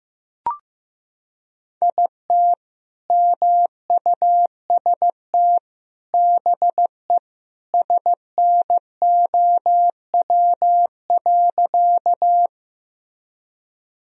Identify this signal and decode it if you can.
Morse code, heard as a keyed tone, about 15 wpm, 700 Hz. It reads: IT MUST BE SNOW.
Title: Sentences from Top 1000 Words - Review Part 3 - 15wpm